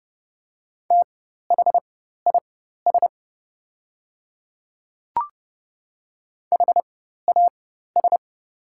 Morse code, heard as a keyed tone, about 30 wpm, spaced as for 14 wpm, 700 Hz. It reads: THIS HAS
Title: 4x Character Spacing - Top 100 Words - 30wpm